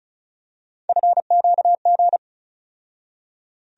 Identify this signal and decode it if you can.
Morse code, heard as a keyed tone, about 36 wpm, 700 Hz. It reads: FQG
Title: USA Call Sign Postfixes - 3 Letters 35wpm